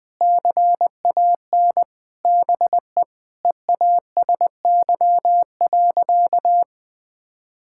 Morse code, heard as a keyed tone, about 20 wpm, 700 Hz. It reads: CAN BE EASY.
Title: Sentences with 2-Word Phrases - 20wpm